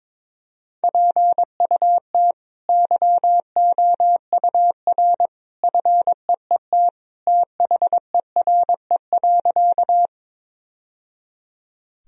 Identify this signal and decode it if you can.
Morse code, heard as a keyed tone, about 22 wpm, 700 Hz. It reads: PUT YOUR FEET THERE.
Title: Sentences from Top 1000 Words - Review Part 1 - 22wpm